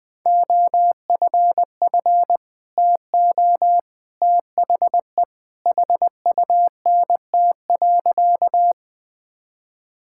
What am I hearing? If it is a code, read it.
Morse code, heard as a keyed tone, about 20 wpm, 700 Hz. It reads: OFF TO THE HUNT.
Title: Sentences from Top 1600 Words 20wpm